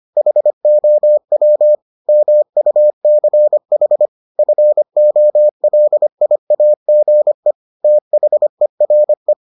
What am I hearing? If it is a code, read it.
Morse code, heard as a keyed tone, about 25 wpm, 600 Hz. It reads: HOW MUCH FOLIAGE THERE